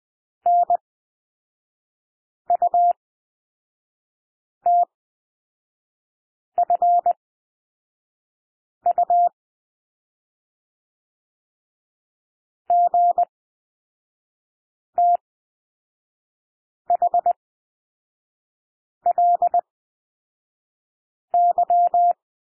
Morse code, heard as a keyed tone, about 20 wpm, spaced as for 5 wpm, 700 Hz. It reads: NUTFU GTHLY
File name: cwmp3 / L2 SOUND LETTERS 20 wpm